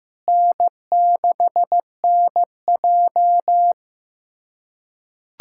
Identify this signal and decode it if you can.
Morse code, heard as a keyed tone, about 15 wpm, 700 Hz. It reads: N6NJ
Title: USA Callsign DE Callsign - 4 Characters - 15wpm